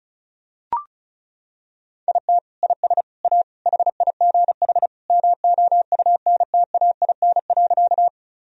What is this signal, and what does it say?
Morse code, heard as a keyed tone, about 36 wpm, 700 Hz. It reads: IT IS A HIGH MOUNTAIN.
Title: Sentences from Top 1000 Words - Review Part 1 - 35wpm